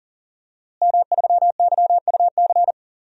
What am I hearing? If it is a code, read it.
Morse code, heard as a keyed tone, about 40 wpm, 700 Hz. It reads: M3YUC
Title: DXCC-100 Call Signs - 5 Characters - Repeated 3x - 40wpm